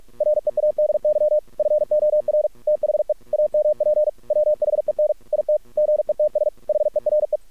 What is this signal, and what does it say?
Morse code, heard as a keyed tone, about 46 wpm, 600 Hz. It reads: READY FOR THE AMP WHEN IT GETS HERE
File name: archived / CRSnet